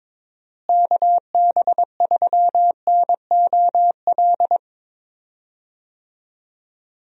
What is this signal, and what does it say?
Morse code, heard as a keyed tone, about 22 wpm, 700 Hz. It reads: KB3NOL
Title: USA Call Signs - 6 Characters - Repeated 2x - 22wpm